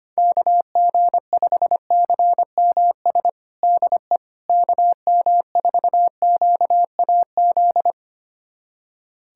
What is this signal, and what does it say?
Morse code, heard as a keyed tone, about 25 wpm, 700 Hz. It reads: KG5CMS DE KM4QAZ
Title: USA Callsign DE Callsign - 6 Characters - 25wpm